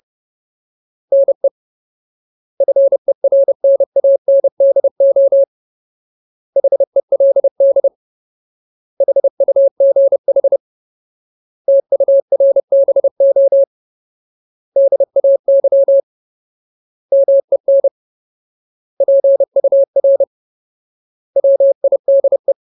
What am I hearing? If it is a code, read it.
Morse code, heard as a keyed tone, about 30 wpm, 550 Hz. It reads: NE FERNANDO HELD HUGH TURBO DAY MEN PUR WIDE
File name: Ultimate Word List / 30WPM Ultimate